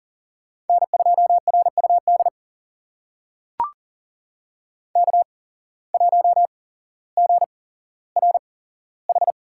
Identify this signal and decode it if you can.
Morse code, heard as a keyed tone, about 40 wpm, spaced as for 17 wpm, 700 Hz. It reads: N2RUD K1GRH
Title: USA Call Signs - 8x Character Spacing - 40wpm